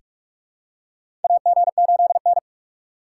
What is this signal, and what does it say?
Morse code, heard as a keyed tone, about 47 wpm, 700 Hz. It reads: AG8N
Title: USA Call Signs - 4 Characters - 2x Character Spacing - 45wpm